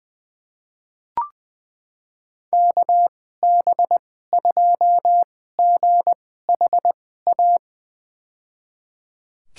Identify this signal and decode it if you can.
Morse code, heard as a keyed tone, about 20 wpm, spaced as for 14 wpm, 700 Hz. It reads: KB2GHA
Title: USA Call Signs - 6 Characters - 2x Character Spacing - 20wpm